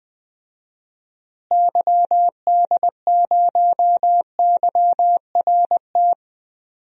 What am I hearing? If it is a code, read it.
Morse code, heard as a keyed tone, about 20 wpm, 700 Hz. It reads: YD0YRT
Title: DXCC-100 Call Signs - 6 Characters - Repeated 2x - 20wpm